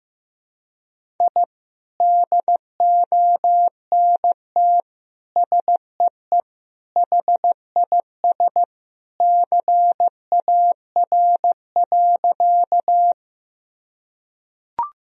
Morse code, heard as a keyed tone, about 16 wpm, 700 Hz. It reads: I DONT SEE HIS CAR.
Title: Sentences from Top 1000 Words - Review Part 1 - 15wpm